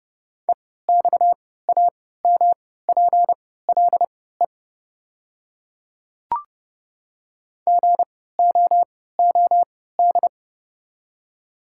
Morse code, heard as a keyed tone, about 29 wpm, spaced as for 17 wpm, 690 Hz. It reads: EXAMPLE GOOD
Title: 3x Character Spacing - Top 300 Words - 30wpm